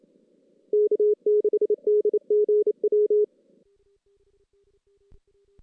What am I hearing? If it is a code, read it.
Morse code, heard as a keyed tone, about 28 wpm, 440 Hz. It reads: K6DGW
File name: 2016 JAN CW NAQP / 2016 Jan naqp k5ot - 0-5